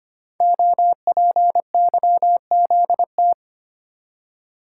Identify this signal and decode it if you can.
Morse code, heard as a keyed tone, about 25 wpm, 700 Hz. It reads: OPYZT
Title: Code Groups - 5 Letters - 25wpm